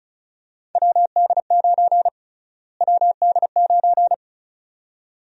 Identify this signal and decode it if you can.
Morse code, heard as a keyed tone, about 36 wpm, 700 Hz. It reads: WD9 WD9
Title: USA Call Sign Prefixes - Repeated 2x - 35wpm